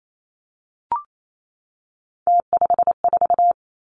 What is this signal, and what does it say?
Morse code, heard as a keyed tone, about 28 wpm, 700 Hz. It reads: T54
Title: Three Challenging Letter-Number Combinations 28wpm